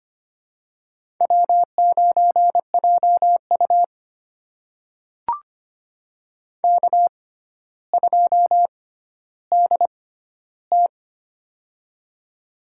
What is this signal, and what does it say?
Morse code, heard as a keyed tone, about 25 wpm, spaced as for 13 wpm, 700 Hz. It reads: W9JU K2DT
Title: USA Call Signs - 4 Characters - 6x Character Spacing - 25wpm